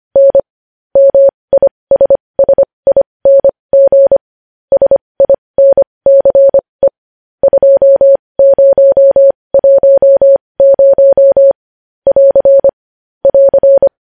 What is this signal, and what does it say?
Morse code, heard as a keyed tone, about 25 wpm, spaced as for 20 wpm, 550 Hz. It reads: N MISSING SINCE 2010 <AR> <AR>
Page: news-headlines-in-morse-code-at-20-wpm